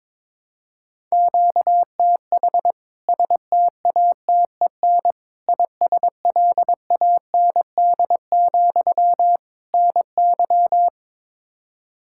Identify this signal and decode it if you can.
Morse code, heard as a keyed tone, about 22 wpm, 700 Hz. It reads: QTH STATEN ISLAND, NY
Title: K1USN SST QSO Practice 22wpm